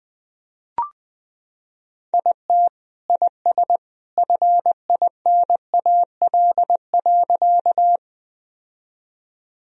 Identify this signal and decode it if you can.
Morse code, heard as a keyed tone, about 20 wpm, 700 Hz. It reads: IT IS FINAL.
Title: Sentences from Top 1000 Words - Review Part 2 - 20wpm